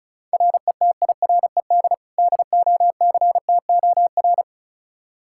Morse code, heard as a keyed tone, about 34 wpm, 700 Hz. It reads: RETIRED DOCTOR
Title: Practice QSO - Exchange 3 - More Info - 35wpm